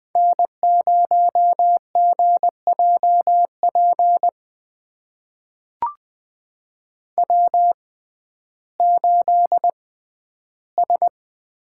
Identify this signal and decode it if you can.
Morse code, heard as a keyed tone, about 20 wpm, spaced as for 12 wpm, 700 Hz. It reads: N0GJP W8S
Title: USA Call Signs - 5 Characters - 6x Character Spacing - 20wpm